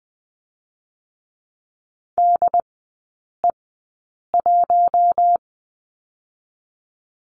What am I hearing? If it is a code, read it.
Morse code, heard as a keyed tone, about 20 wpm, 700 Hz. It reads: D E 1
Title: 100 Lesson 17e - Letter u - Three Characters 20wpm